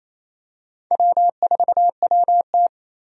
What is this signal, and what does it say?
Morse code, heard as a keyed tone, about 27 wpm, 700 Hz. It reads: W4WT
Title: USA Call Signs - 4 Characters - 28wpm